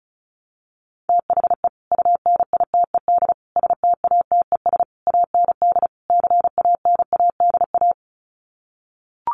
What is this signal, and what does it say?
Morse code, heard as a keyed tone, about 34 wpm, 700 Hz. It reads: THE UNITED STATES AND CANADA
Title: Sets of 5 Words 35wpm